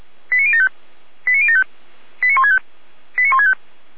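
Five-tone selective call (ZVEI), heard as quick runs of five tones, tones of about 80 ms. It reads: 89075 89075 89165 89165
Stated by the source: Five-tone ZVEI selective call received on an SDR, channel 455, archived by Feuerwehr Orlamünde.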